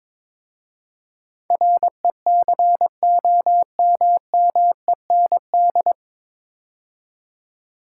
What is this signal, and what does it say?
Morse code, heard as a keyed tone, about 22 wpm, 700 Hz. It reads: RECOMMEND
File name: 9 Letter Words - First 3 Letters Given - 22wpm